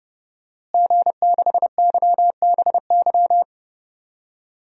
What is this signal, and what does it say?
Morse code, heard as a keyed tone, about 30 wpm, 700 Hz. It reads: G6YBY
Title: DXCC-100 Call Signs - 5 Characters - Repeated 2x - 30wpm